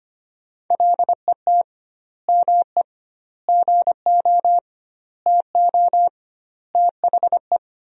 Morse code, heard as a keyed tone, about 25 wpm, 700 Hz. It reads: LET ME GO TO THE
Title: 2x Word-Spacing - Sentences from Top 300 Words - 25wpm